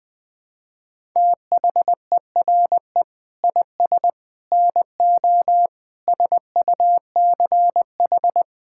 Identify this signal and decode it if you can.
Morse code, heard as a keyed tone, about 20 wpm, 700 Hz. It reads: THERE IS NO SUCH
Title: Sets of 6 Words - 20wpm